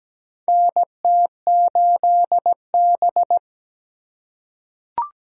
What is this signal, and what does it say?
Morse code, heard as a keyed tone, about 17 wpm, 700 Hz. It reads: NT8B
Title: USA Call Signs - 4 Characters - 2x Character Spacing - 17wpm